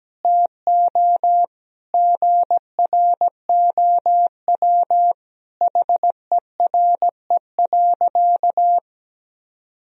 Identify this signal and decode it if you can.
Morse code, heard as a keyed tone, about 17 wpm, 700 Hz. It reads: TO GROW HERE.
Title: Sentences from Top 1000 Words - Review Part 1 - 17wpm